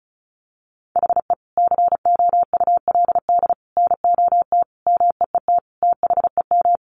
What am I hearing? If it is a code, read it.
Morse code, heard as a keyed tone, about 34 wpm, 700 Hz. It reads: HE COULD NOT MEET THEM
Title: Sentences from Top 1100 Words 35wpm